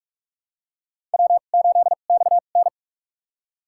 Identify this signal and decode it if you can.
Morse code, heard as a keyed tone, about 46 wpm, spaced as for 33 wpm, 700 Hz. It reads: W8XN
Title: USA Call Signs - 4 Characters - 2x Character Spacing - 45wpm